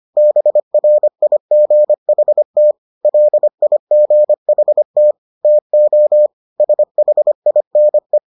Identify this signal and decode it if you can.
Morse code, heard as a keyed tone, about 25 wpm, 600 Hz. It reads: BRIGHT LIGHT TO SHINE